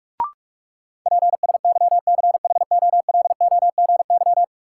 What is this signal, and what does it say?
Morse code, heard as a keyed tone, about 46 wpm, 700 Hz. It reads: PSYCHOLOGY
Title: Sentences from Top 1400 Words - New Word Drill Alt - 45wpm